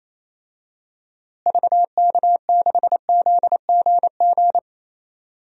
Morse code, heard as a keyed tone, about 28 wpm, 700 Hz. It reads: VK6ZGG
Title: DXCC-100 Call Signs - 6 Characters - Repeated 2x - 28wpm